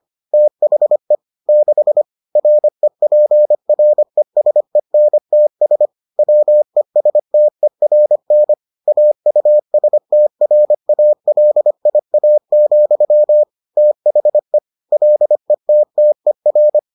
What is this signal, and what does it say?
Morse code, heard as a keyed tone, about 25 wpm, 600 Hz. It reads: THE 6 REPRESENTS WESTERN AUSTRALIA, THE LETTER